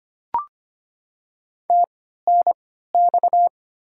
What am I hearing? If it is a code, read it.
Morse code, heard as a keyed tone, about 25 wpm, spaced as for 14 wpm, 700 Hz. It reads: TNX
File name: Top 20 QSO Elements - 3x Character Spacing - 25wpm